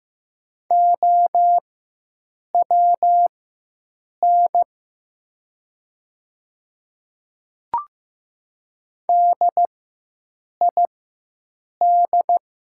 Morse code, heard as a keyed tone, about 15 wpm, spaced as for 7 wpm, 700 Hz. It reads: OWN DID
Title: 4x Character Spacing - Top 300 Words - 15wpm